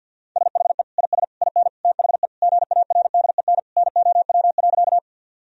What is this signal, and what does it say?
Morse code, heard as a keyed tone, about 49 wpm, 700 Hz. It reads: SHE IS IN THE GARDEN NOW.
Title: Sentences from Top 1000 Words - Review Part 4 - 50wpm